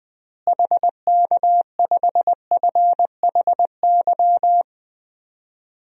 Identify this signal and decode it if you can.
Morse code, heard as a keyed tone, about 21 wpm, 700 Hz. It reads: HK5FHY
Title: DXCC-100 Call Signs - 6 Characters - Repeated 2x - 20wpm